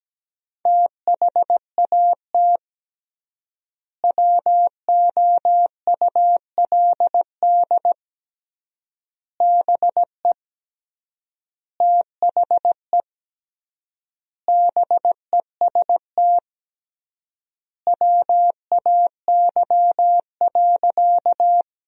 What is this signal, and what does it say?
Morse code, heard as a keyed tone, about 17 wpm, 700 Hz. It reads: THAT WOULD BE THE BEST WAY.
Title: 3x Word-Spacing - Sentences from Top 400 Words - 17wpm